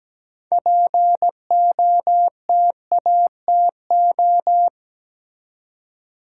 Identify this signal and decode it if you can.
Morse code, heard as a keyed tone, about 17 wpm, 700 Hz. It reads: POTATO
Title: Six Letter Words - 17wpm